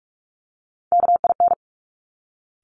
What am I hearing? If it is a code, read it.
Morse code, heard as a keyed tone, about 46 wpm, 690 Hz. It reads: KIN
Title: Top Three Letter Combinations in Words 45wpm